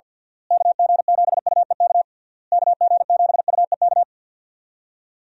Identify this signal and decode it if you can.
Morse code, heard as a keyed tone, about 51 wpm, 700 Hz. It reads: KG7UEK KG7UEK
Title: USA Call Signs - 6 Characters - Repeated 3x - 50wpm